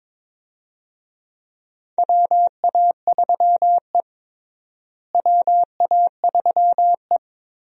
Morse code, heard as a keyed tone, about 21 wpm, 700 Hz. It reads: WA3E WA3E
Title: USA Call Signs - 4 Characters - Repeated 2x - 22wpm